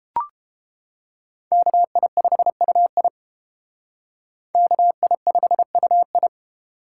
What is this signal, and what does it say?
Morse code, heard as a keyed tone, about 31 wpm, 700 Hz. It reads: KI5UI KI5UI